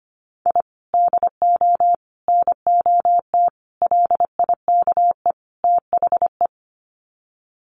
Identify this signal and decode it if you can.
Morse code, heard as a keyed tone, about 25 wpm, 700 Hz. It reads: I DO NOT LIKE THE
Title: Sets of 5 Words Encore 25wpm